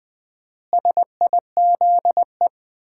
Morse code, heard as a keyed tone, about 20 wpm, 700 Hz. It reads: SIZE
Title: 4x Word-Spacing - Sets of 5 Words - 20wpm